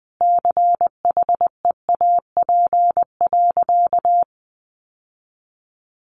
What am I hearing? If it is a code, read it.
Morse code, heard as a keyed tone, about 20 wpm, 700 Hz. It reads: CHEAP.
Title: Sentences from Top 1200 Words 20wpm